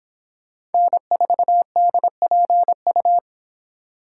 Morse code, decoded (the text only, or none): N4DPU